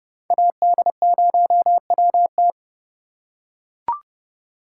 Morse code, heard as a keyed tone, about 30 wpm, 700 Hz. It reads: AD0WT